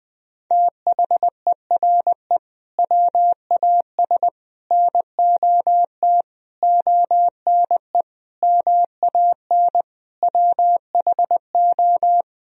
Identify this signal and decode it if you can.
Morse code, heard as a keyed tone, about 20 wpm, 700 Hz. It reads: THERE WAS NOT ONE MAN WHO
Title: Sentences from Top 1000 Words - Review Part 2 - 20wpm